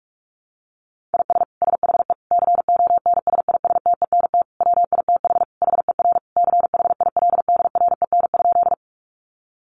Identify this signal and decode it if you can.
Morse code, heard as a keyed tone, about 44 wpm, 700 Hz. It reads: IS SHE CONSISTENT WITH HER CHILDREN?
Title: Sentences from Top 1500 Words 45wpm